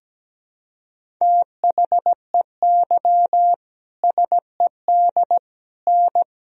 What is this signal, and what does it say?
Morse code, heard as a keyed tone, about 17 wpm, 700 Hz. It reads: THEY SED N
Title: Sentences from Top 100 Words and Abbreviations 17wpm